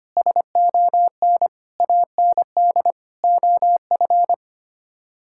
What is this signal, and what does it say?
Morse code, heard as a keyed tone, about 25 wpm, 700 Hz. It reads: SON AND OF
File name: Sets of 9 Words - 25wpm